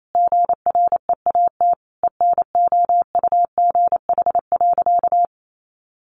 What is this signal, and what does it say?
Morse code, heard as a keyed tone, about 28 wpm, 700 Hz. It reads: GREAT ENOUGH.
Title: Sentences from Top 1500 Words 28wpm